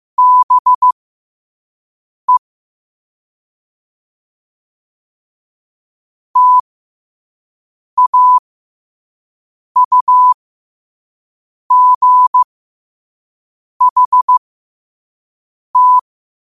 Morse code, heard as a keyed tone, about 15 wpm, spaced as for 5 wpm, 1000 Hz. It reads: BE TAUGHT
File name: qotdfiles / 5wpm